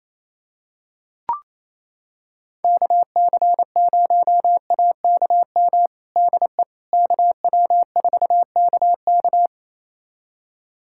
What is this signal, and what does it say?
Morse code, heard as a keyed tone, about 28 wpm, 700 Hz. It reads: KC0AKM DE KW4KK
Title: USA Callsign DE Callsign 28wpm